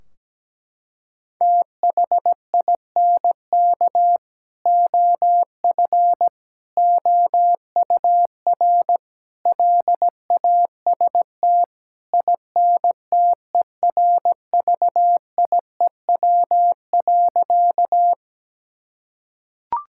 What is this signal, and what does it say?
Morse code, heard as a keyed tone, about 17 wpm, 700 Hz. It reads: THINK OF OUR LAST INTERVIEW.
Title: Sentences from Top 1600 Words 17wpm